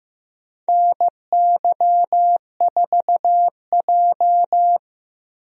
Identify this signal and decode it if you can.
Morse code, heard as a keyed tone, about 15 wpm, 700 Hz. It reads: NY4J
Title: USA Call Signs - 4 Characters - Repeated 3x - 15wpm